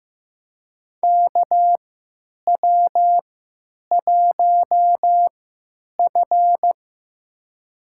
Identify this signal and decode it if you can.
Morse code, heard as a keyed tone, about 15 wpm, spaced as for 9 wpm, 700 Hz. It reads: KW1F
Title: USA Call Signs - 4 Characters - 3x Character Spacing - 15wpm